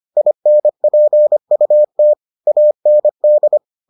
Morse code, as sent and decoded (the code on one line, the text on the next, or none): .. -. .--. ..- - / .- -. -..
INPUT AND